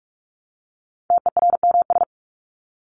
Morse code, heard as a keyed tone, about 44 wpm, 700 Hz. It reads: TERMS
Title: ES:S Suffix - Top 100 Words with Base Words 45wpm